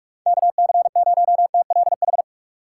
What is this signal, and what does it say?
Morse code, heard as a keyed tone, about 46 wpm, 700 Hz. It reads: KC0TLH
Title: USA Call Signs - 6 Characters - 4x Character Spacing - 45wpm